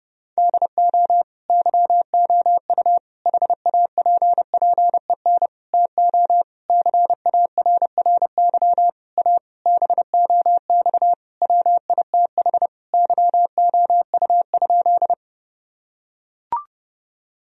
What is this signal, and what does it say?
Morse code, heard as a keyed tone, about 31 wpm, 700 Hz. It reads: DO YOU HAPPEN TO CARRY A BOX WITH YOU?
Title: Sentences from Top 1000 Words - Review Part 2 - 30wpm